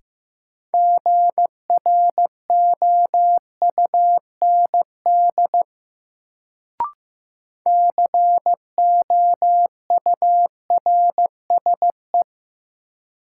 Morse code, heard as a keyed tone, about 15 wpm, 700 Hz. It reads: GROUND COURSE
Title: Sentences from Top 500 Words - New Word Drill - 15wpm